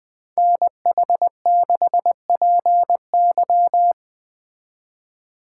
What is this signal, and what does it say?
Morse code, heard as a keyed tone, about 20 wpm, 700 Hz. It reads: NH6PY
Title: Speed Racing - USA 2x2 Call Signs 17wpm